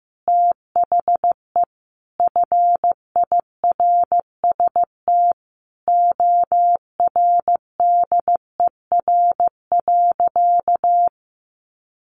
Morse code, heard as a keyed tone, about 15 wpm, 700 Hz. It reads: THE FIRST ORDER.
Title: Sentences from Top 1100 Words 15wpm